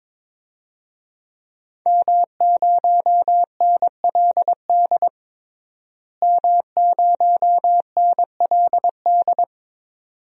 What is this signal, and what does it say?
Morse code, heard as a keyed tone, about 21 wpm, 700 Hz. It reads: M0NLD M0NLD
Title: DXCC-100 Call Signs - 5 Characters - Repeated 2x - 22wpm